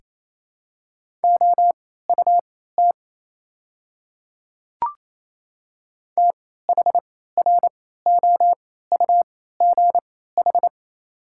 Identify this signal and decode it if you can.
Morse code, heard as a keyed tone, about 28 wpm, spaced as for 16 wpm, 700 Hz. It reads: OUT THROUGH
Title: 3x Character Spacing - Top 200 Words - 28wpm